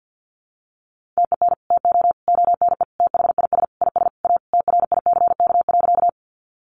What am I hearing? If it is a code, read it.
Morse code, heard as a keyed tone, about 48 wpm, 700 Hz. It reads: TEN TO ONE THIS IS A TRICK.
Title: Sentences from Top 1500 Words 50wpm